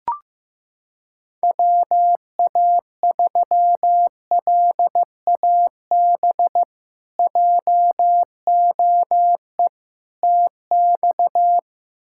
Morse code, heard as a keyed tone, about 16 wpm, 700 Hz. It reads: WA3LAB JOE TX
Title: K1USN SST QSO Practice - 15wpm